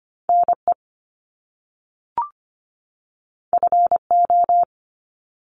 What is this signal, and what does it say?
Morse code, heard as a keyed tone, about 25 wpm, 700 Hz. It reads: NE FO
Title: Top Two Letter Combinations in Words 25wpm